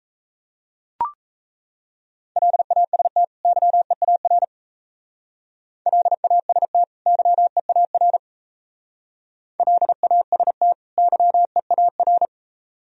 Morse code, heard as a keyed tone, about 37 wpm, 700 Hz. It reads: LAST YEAR LAST YEAR LAST YEAR